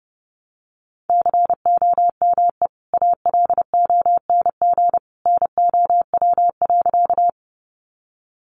Morse code, heard as a keyed tone, about 30 wpm, 700 Hz. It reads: COME ALONG NOW.
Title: Sentences from Top 700 Words 30wpm